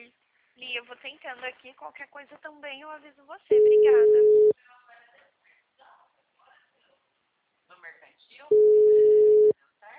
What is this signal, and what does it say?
Ringback tone (Europe, 425 Hz), with speech.